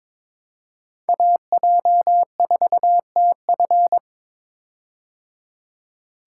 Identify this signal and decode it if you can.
Morse code, heard as a keyed tone, about 22 wpm, 700 Hz. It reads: AJ4TF
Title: K1USN SST QSO Practice - 22wpm